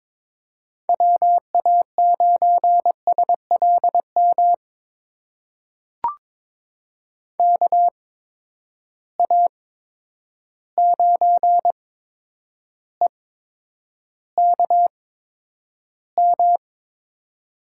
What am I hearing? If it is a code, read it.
Morse code, heard as a keyed tone, about 22 wpm, spaced as for 9 wpm, 700 Hz. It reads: WA9SLM KA9EKM
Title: USA Call Signs - 6 Characters - 8x Character Spacing - 22wpm